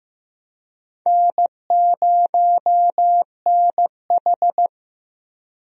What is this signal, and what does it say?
Morse code, heard as a keyed tone, about 15 wpm, 700 Hz. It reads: N0NH
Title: USA Call Signs - 4 Characters - Repeated 2x - 15wpm